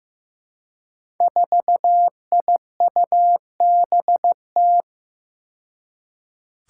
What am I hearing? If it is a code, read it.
Morse code, heard as a keyed tone, about 15 wpm, 700 Hz. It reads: 4IUBT
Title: Five Challenging Letter-Number Combinations 15wpm